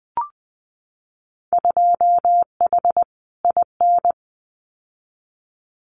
Morse code, heard as a keyed tone, about 20 wpm, 700 Hz. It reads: 2H IN
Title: Winter Field Day Exchanges Encore 20wpm